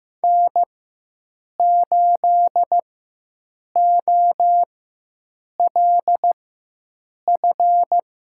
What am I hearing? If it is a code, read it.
Morse code, heard as a keyed tone, about 15 wpm, spaced as for 7 wpm, 700 Hz. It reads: N8OLF
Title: USA Call Signs - 4x Character Spacing - 15wpm